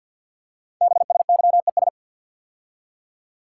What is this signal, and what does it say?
Morse code, heard as a keyed tone, about 48 wpm, 700 Hz. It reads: BSYES
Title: Code Groups - 5 Letters - 50wpm